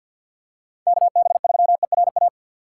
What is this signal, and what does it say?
Morse code, heard as a keyed tone, about 53 wpm, 700 Hz. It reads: KB3ERA
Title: USA Call Signs - 6 Characters - 8x Character Spacing - 50wpm